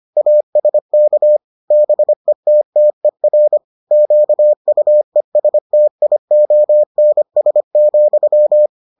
Morse code, heard as a keyed tone, about 25 wpm, 600 Hz. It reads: ASK BETTER QUESTIONS,